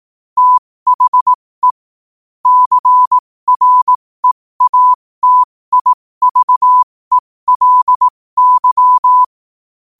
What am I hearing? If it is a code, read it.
Morse code, heard as a keyed tone, about 18 wpm, spaced as for 15 wpm, 1000 Hz. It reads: THE CREATIVELY
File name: qotdfiles / 15wpm